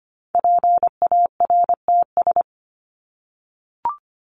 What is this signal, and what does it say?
Morse code, heard as a keyed tone, about 25 wpm, 700 Hz. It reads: PARTS
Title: ES:S Suffix - Top 100 Words with Base Words 25wpm